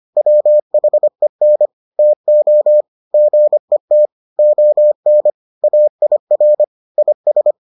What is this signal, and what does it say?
Morse code, heard as a keyed tone, about 25 wpm, 600 Hz. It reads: WHEN TO GET ON AIR IS